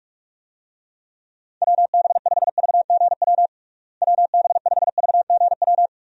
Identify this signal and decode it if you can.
Morse code, heard as a keyed tone, about 46 wpm, 700 Hz. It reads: WB5VGW WB5VGW
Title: USA Call Signs - 6 Characters - Repeated 3x - 45wpm